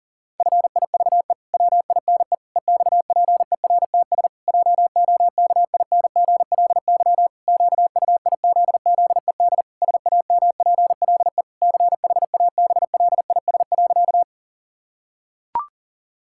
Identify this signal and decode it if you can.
Morse code, heard as a keyed tone, about 40 wpm, 700 Hz. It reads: FIVE WINE EXPERTS JOKINGLY QUIZZED SAMPLE CHABLIS.
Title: Pangram Challenge v2 - 40wpm